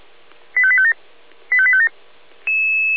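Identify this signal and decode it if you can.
Five-tone selective call (ZVEI), heard as quick runs of five tones, tones of about 70 ms. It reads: 85857 85857
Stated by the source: ZVEI five-tone call for address 85810, received by SDR.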